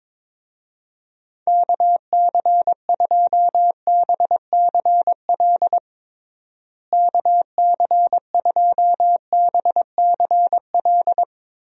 Morse code, heard as a keyed tone, about 22 wpm, 700 Hz. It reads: KC2BCL KC2BCL
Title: USA Call Signs - 6 Characters - Repeated 2x - 22wpm